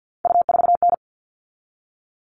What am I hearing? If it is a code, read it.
Morse code, heard as a keyed tone, about 50 wpm, 700 Hz. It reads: U4N
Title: All Cut Numbers - 3-Digit 50wpm